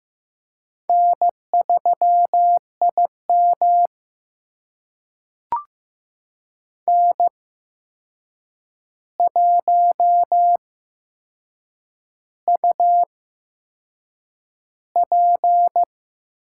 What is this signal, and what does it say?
Morse code, heard as a keyed tone, about 15 wpm, spaced as for 6 wpm, 700 Hz. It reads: N3IM N1UP
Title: USA Call Signs - 4 Characters - 8x Character Spacing - 15wpm